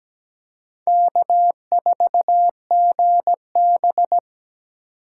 Morse code, heard as a keyed tone, about 17 wpm, 700 Hz. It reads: K4GB
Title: USA Call Signs - 4 Characters - 3x Character Spacing - 17wpm